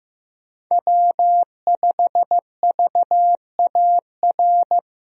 Morse code, heard as a keyed tone, about 15 wpm, 700 Hz. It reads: W5VAR
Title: USA Call Signs - 5 Characters - Repeated 3x - 15wpm